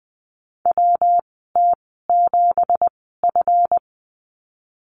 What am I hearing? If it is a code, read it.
Morse code, heard as a keyed tone, about 20 wpm, spaced as for 15 wpm, 700 Hz. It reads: WT7F
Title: 173 Lesson 28g - Number 9 - Call Signs 20wpm